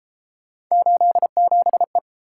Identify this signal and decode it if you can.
Morse code, heard as a keyed tone, about 31 wpm, 700 Hz. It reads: 87E